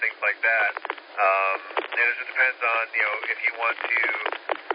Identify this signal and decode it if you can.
Emergency Alert System (SAME) tones, heard as harsh buzzing bursts, with speech. It is heard from 0.3 to 4.2 s.